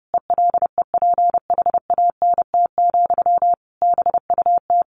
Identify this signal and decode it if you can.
Morse code, heard as a keyed tone, about 30 wpm, 700 Hz. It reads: ELEPHANT, BUT
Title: A Sound Of Thunder 30wpm